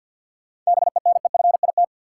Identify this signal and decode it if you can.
Morse code, heard as a keyed tone, about 48 wpm, 700 Hz. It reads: BENEFIT
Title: Sets of 5 Words Encore - New Word Drill - 50wpm